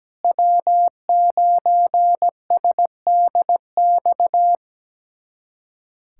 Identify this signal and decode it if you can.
Morse code, heard as a keyed tone, about 17 wpm, 700 Hz. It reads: W9SDX
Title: USA Callsign DE Callsign - 5 Characters - 17wpm